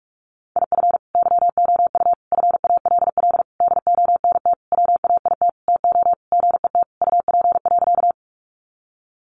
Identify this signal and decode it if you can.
Morse code, heard as a keyed tone, about 44 wpm, 700 Hz. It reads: IF YOU FALL DONT WAIT TO GET UP.
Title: Sentences from Top 500 Words 45wpm